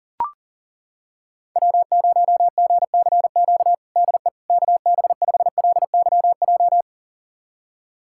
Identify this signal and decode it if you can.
Morse code, heard as a keyed tone, about 37 wpm, 700 Hz. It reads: W0GCQ DE KB5LYJ
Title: USA Callsign DE Callsign 40wpm